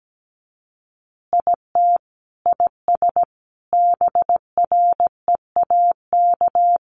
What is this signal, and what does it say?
Morse code, heard as a keyed tone, about 17 wpm, 700 Hz. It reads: IT IS BREAK
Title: Easy Phrases 17wpm